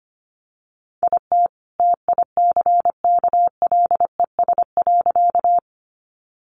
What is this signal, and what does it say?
Morse code, heard as a keyed tone, about 25 wpm, 700 Hz. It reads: IT TICKLES.
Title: Additional Easy Phrases 25wpm